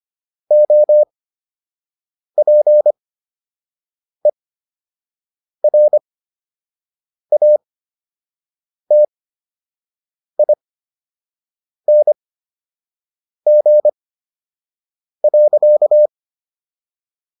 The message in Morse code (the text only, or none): OPERATING.